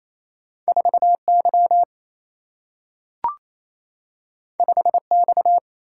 Morse code, heard as a keyed tone, about 28 wpm, 700 Hz. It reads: 4Y 5X
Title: Single-Letter Single-Number 28wpm